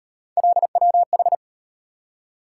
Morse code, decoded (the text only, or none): LWH